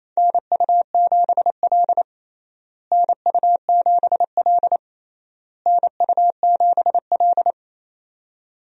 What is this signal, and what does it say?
Morse code, heard as a keyed tone, about 28 wpm, 700 Hz. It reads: NU7L NU7L NU7L
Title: USA Call Signs - 4 Characters - Repeated 3x - 28wpm